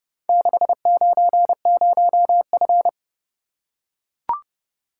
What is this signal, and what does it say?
Morse code, heard as a keyed tone, about 30 wpm, 700 Hz. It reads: 690F